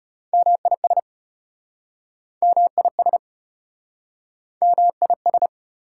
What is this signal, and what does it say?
Morse code, heard as a keyed tone, about 33 wpm, 700 Hz. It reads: MIS MIS MIS